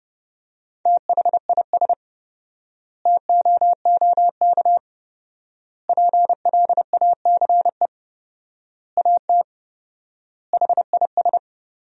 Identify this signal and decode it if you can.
Morse code, heard as a keyed tone, about 30 wpm, 700 Hz. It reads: THIS TOOK PLACE AT HIS
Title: 4x Word-Spacing - Sentences from Top 300 Words - 30wpm